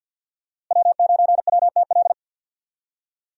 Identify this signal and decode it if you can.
Morse code, heard as a keyed tone, about 53 wpm, 700 Hz. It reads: W9WTL
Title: USA Call Signs - 5 Characters - Repeated 2x - 50wpm